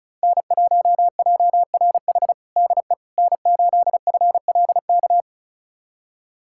Morse code, heard as a keyed tone, about 34 wpm, 700 Hz. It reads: N1JRH DE N8FLK
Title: USA Callsign DE Callsign - 5 Characters - 35wpm